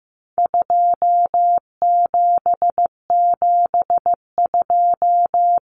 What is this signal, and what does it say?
Morse code, heard as a keyed tone, about 15 wpm, 700 Hz. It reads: 2772
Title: Common Cut Numbers - 4-Digit 15wpm